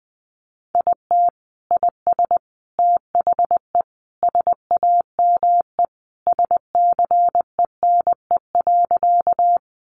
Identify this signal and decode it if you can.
Morse code, heard as a keyed tone, about 20 wpm, 700 Hz. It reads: IT IS THE SAME SCENE.
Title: Sentences from Top 1100 Words 20wpm